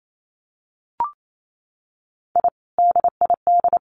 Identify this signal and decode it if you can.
Morse code, heard as a keyed tone, about 28 wpm, 700 Hz. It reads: I DID
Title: Sets of 2 Words 28wpm